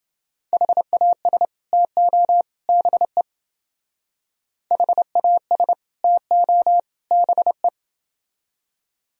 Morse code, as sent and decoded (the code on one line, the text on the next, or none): .... .- ... / - --- / -... / .... .- ... / - --- / -... .
HAS TO BE HAS TO BE